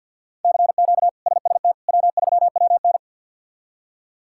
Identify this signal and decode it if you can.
Morse code, heard as a keyed tone, about 49 wpm, 700 Hz. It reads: CQ SST W3WN
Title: K1USN SST QSO Practice - 50wpm